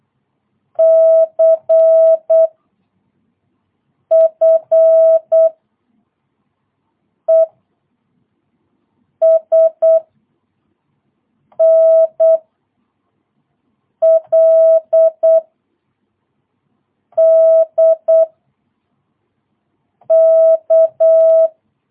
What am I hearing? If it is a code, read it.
Morse code, heard as a keyed tone, about 8 wpm, 660 Hz. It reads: C F E S N L D K
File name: jh8cbh / 2022cw